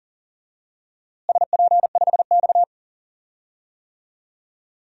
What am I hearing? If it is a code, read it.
Morse code, heard as a keyed tone, about 40 wpm, 700 Hz. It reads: SP5X
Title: DXCC-100 Call Signs - 4 Characters - Repeated 2x - 40wpm